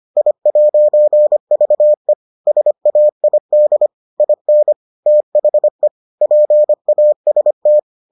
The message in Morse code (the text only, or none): I'VE SAID IN THE PAST